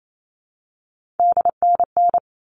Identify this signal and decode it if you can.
Morse code, heard as a keyed tone, about 28 wpm, 700 Hz. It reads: DNN
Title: All Cut Numbers - 3-Digit 28wpm